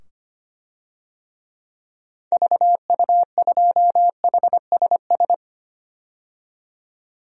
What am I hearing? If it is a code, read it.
Morse code, heard as a keyed tone, about 25 wpm, 700 Hz. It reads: VU2HSS
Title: DXCC-100 Call Signs - 6 Characters - 25wpm